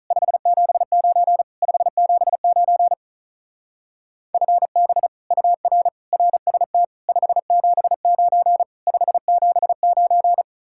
Morse code, heard as a keyed tone, about 44 wpm, 700 Hz. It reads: 579 579 FB UR RST 579 579